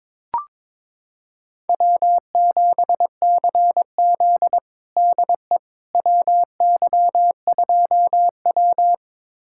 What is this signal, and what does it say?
Morse code, heard as a keyed tone, about 22 wpm, 700 Hz. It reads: W7CZ DE WY2W
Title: USA Callsign DE Callsign - 4 Characters - 22wpm